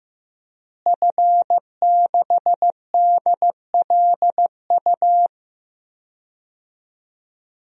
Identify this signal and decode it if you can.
Morse code, heard as a keyed tone, about 15 wpm, 700 Hz. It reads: F6DLU
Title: DXCC-100 Call Signs - 5 Characters - Repeated 2x - 15wpm